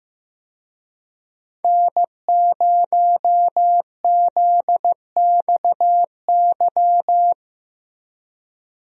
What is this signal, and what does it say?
Morse code, heard as a keyed tone, about 15 wpm, 700 Hz. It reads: N0ZXY
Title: USA Call Signs - 5 Characters - Repeated 3x - 15wpm